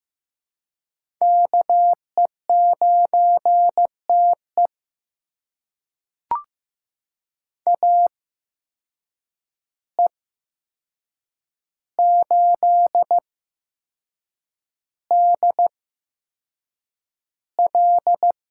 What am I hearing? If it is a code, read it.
Morse code, heard as a keyed tone, about 15 wpm, spaced as for 6 wpm, 700 Hz. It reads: KE9TE AE8DL
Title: USA Call Signs - 5 Characters - 8x Character Spacing - 15wpm